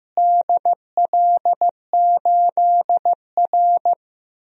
Morse code, heard as a keyed tone, about 15 wpm, 700 Hz. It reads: DL8R
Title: DXCC-100 Call Signs - 4 Characters - Repeated 2x - 15wpm